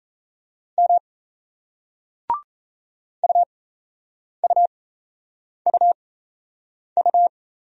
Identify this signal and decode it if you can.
Morse code, heard as a keyed tone, about 35 wpm, spaced as for 8 wpm, 700 Hz. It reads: M UUUU